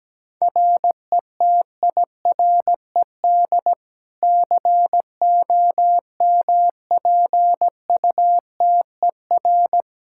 Morse code, heard as a keyed tone, about 17 wpm, 700 Hz. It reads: RETIRED COMPUTER
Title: Practice QSO - Exchange 3 - More Info - 17wpm